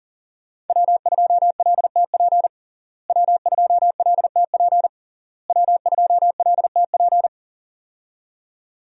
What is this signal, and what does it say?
Morse code, heard as a keyed tone, about 37 wpm, 700 Hz. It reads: W2LTP W2LTP W2LTP
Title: USA Call Signs - 5 Characters - Repeated 3x - 40wpm